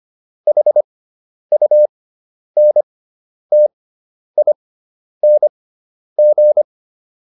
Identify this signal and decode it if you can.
Morse code, heard as a keyed tone, about 25 wpm, spaced as for 10 wpm, 600 Hz. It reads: HUNTING